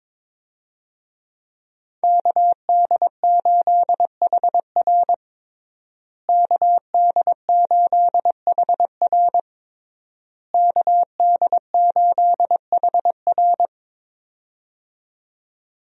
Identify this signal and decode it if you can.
Morse code, heard as a keyed tone, about 22 wpm, 700 Hz. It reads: KD8HR KD8HR KD8HR
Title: USA Call Signs - 5 Characters - Repeated 3x - 22wpm